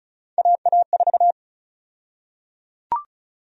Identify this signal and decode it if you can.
Morse code, heard as a keyed tone, about 36 wpm, 700 Hz. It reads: AA4
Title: USA Call Sign Prefixes - 35wpm